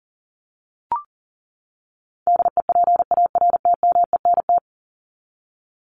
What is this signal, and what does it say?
Morse code, heard as a keyed tone, about 40 wpm, 700 Hz. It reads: DEPARTMENT
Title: Top 751-1000 Words 40wpm